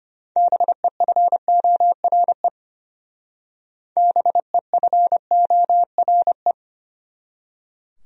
Morse code, heard as a keyed tone, about 32 wpm, 700 Hz. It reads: BEFORE BEFORE